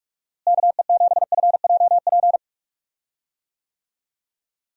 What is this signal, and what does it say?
Morse code, heard as a keyed tone, about 46 wpm, 700 Hz. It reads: KE7FJP
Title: USA Call Signs - 6 Characters - Repeated 3x - 45wpm